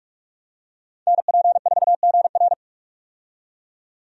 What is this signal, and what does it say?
Morse code, heard as a keyed tone, about 46 wpm, 700 Hz. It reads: NP4GR
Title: USA Call Signs - 5 Characters - 45wpm